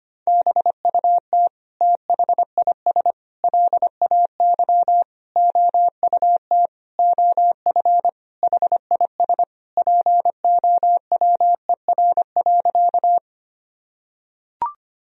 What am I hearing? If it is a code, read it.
Morse code, heard as a keyed tone, about 25 wpm, 700 Hz. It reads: BUT THIS LAY OUT OF HIS POWER.
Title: Sentences from Top 1000 Words - Review Part 2 - 25wpm